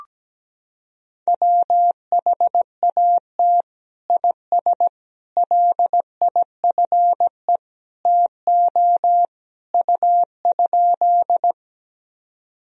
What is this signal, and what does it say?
Morse code, heard as a keyed tone, about 17 wpm, 700 Hz. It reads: WHAT IS LIFE TO U?
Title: Sentences from Top 300 Words and Abbreviations 17wpm